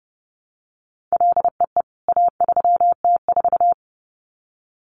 Morse code, heard as a keyed tone, about 28 wpm, 700 Hz. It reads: LEE A3T4
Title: CWT Exchange 30wpm